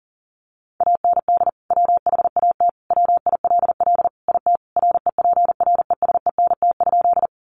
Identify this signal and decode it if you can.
Morse code, heard as a keyed tone, about 40 wpm, 700 Hz. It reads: AND WHAT WILL IT REPRESENT?
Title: Sentences from Top 1200 Words 40wpm